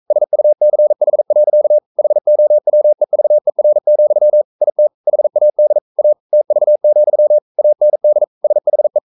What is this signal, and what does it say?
Morse code, heard as a keyed tone, about 42 wpm, 600 Hz. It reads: SUCH. HOWEVER, IT HAD A TV, AND SHE
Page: Romanen Never af Ken Follet på engelsk.